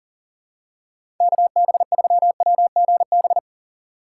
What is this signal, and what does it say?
Morse code, heard as a keyed tone, about 41 wpm, 700 Hz. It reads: KB3WGB
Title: USA Call Signs - 6 Characters - 4x Character Spacing - 40wpm